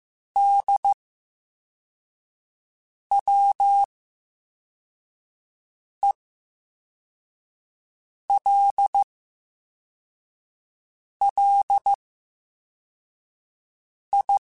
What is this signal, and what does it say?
Morse code, heard as a keyed tone, about 15 wpm, spaced as for 4 wpm, 800 Hz. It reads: DWELLI
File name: Special INGs / Special INGs-mp3s / Slow ing @13